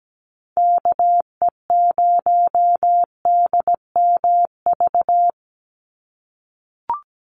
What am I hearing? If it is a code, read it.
Morse code, heard as a keyed tone, about 17 wpm, 700 Hz. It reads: KE0DMV
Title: USA Call Signs 17wpm